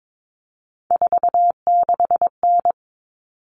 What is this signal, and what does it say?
Morse code, heard as a keyed tone, about 22 wpm, 700 Hz. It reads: 46N
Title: All Cut Numbers - 3-Digit 22wpm